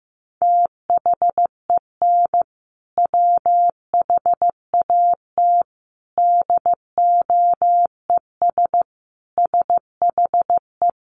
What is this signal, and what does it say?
Morse code, heard as a keyed tone, about 15 wpm, 700 Hz. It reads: THEN WHAT DOES SHE
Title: Sentences from Top 1500 Words 15wpm